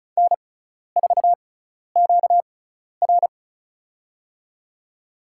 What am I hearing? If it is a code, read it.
Morse code, heard as a keyed tone, about 34 wpm, spaced as for 12 wpm, 700 Hz. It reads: N4QR
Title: USA Call Signs - 4 Characters - 6x Character Spacing - 35wpm